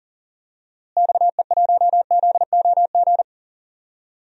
Morse code, heard as a keyed tone, about 40 wpm, 700 Hz. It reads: XE1ZOG
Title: DXCC-100 Call Signs - 6 Characters - Repeated 3x - 40wpm